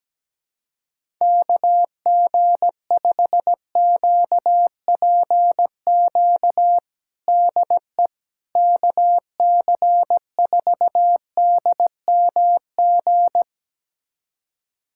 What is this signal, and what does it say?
Morse code, heard as a keyed tone, about 17 wpm, 700 Hz. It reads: KG5QPQ DE KC4DMG
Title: USA Callsign DE Callsign - 6 Characters - 17wpm